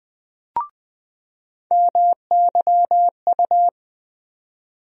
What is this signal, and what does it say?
Morse code, heard as a keyed tone, about 20 wpm, 700 Hz. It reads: MYU
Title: USA Call Sign Postfixes - 3 Letters 20wpm